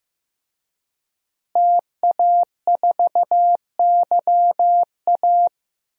Morse code, heard as a keyed tone, about 16 wpm, 700 Hz. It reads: TA4YA
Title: DXCC-100 Call Signs - 5 Characters - Repeated 3x - 15wpm